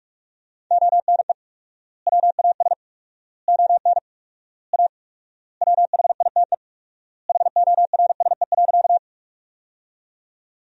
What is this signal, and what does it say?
Morse code, heard as a keyed tone, about 42 wpm, 700 Hz. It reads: ONE WAS ON A WHITE HORSE.
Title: 4x Word-Spacing - Sentences from Top 300 Words - 45wpm